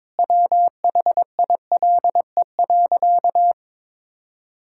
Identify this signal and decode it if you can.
Morse code, heard as a keyed tone, about 22 wpm, 700 Hz. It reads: WHILE.
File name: practice / 22wpm / Sentences with 4-Word Phrases - 22wpm